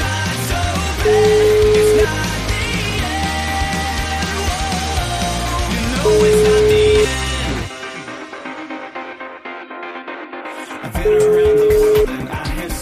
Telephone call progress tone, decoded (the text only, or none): ringback tone